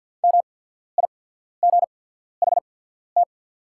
Morse code, heard as a keyed tone, about 48 wpm, spaced as for 14 wpm, 700 Hz. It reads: MIGHT
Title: 8x Character Spacing - Top 300 Words - 50wpm